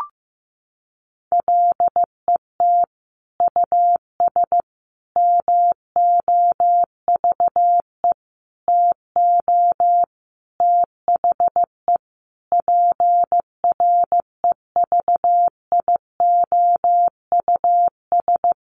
Morse code, heard as a keyed tone, about 14 wpm, 700 Hz. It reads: LET US MOVE TO THE PREVIOUS
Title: Sentences from Top 1500 Words 15wpm